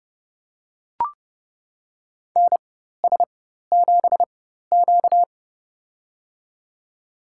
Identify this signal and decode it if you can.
Morse code, heard as a keyed tone, about 30 wpm, spaced as for 14 wpm, 700 Hz. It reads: NS7Q
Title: USA Call Signs - 4 Characters - 4x Character Spacing - 30wpm